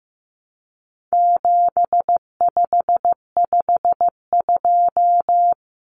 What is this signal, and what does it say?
Morse code, heard as a keyed tone, about 15 wpm, 700 Hz. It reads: 7552
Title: Common Cut Numbers - 4-Digit 15wpm